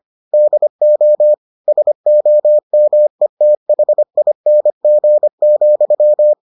Morse code, heard as a keyed tone, about 25 wpm, 600 Hz. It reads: DO SOMETHING,